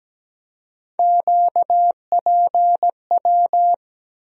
Morse code, heard as a keyed tone, about 17 wpm, 700 Hz. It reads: QPW
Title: USA Call Sign Postfixes - 3 Letters 17wpm